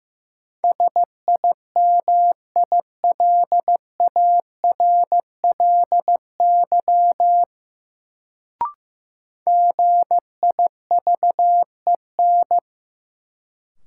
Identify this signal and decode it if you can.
Morse code, heard as a keyed tone, about 15 wpm, 700 Hz. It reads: SIMILARLY GIVEN
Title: Sentences from Top 1400 Words - New Word Drill - 15wpm